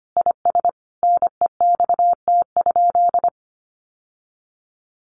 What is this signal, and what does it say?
Morse code, heard as a keyed tone, about 25 wpm, 700 Hz. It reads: IS NEXT?
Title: Sentences from Top 1200 Words 25wpm